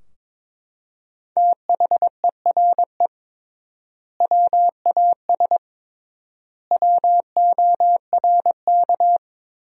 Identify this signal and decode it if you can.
Morse code, heard as a keyed tone, about 21 wpm, 700 Hz. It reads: THERE WAS WORK
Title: 3x Word-Spacing - Sentences from Top 200 Words - 22wpm